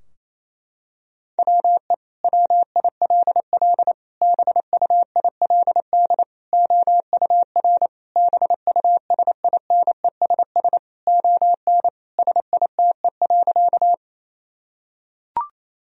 Morse code, heard as a keyed tone, about 28 wpm, 700 Hz. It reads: WE WILL BUILD OUR BUSINESS ON SITE.
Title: Sentences from Top 1000 Words - Review Part 4 - 28wpm